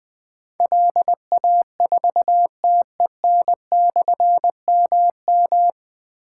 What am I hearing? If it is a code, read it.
Morse code, heard as a keyed tone, about 21 wpm, 700 Hz. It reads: LA4TEN/MM
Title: Difficult Call Signs 20wpm